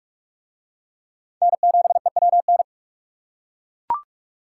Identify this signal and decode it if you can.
Morse code, heard as a keyed tone, about 42 wpm, 700 Hz. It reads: N7EWN